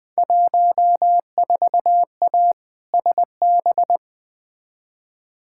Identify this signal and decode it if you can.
Morse code, heard as a keyed tone, about 20 wpm, 700 Hz. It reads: 14A SB
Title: Field Day Exchanges Encore - 20wpm